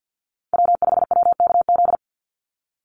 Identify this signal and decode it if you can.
Morse code, heard as a keyed tone, about 50 wpm, 700 Hz. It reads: W5WKZ
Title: USA Call Signs 50wpm